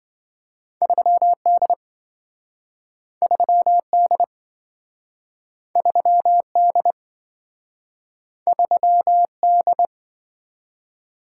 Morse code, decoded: 3D 3D 3D 3D

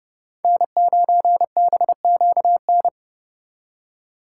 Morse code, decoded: N9BQN

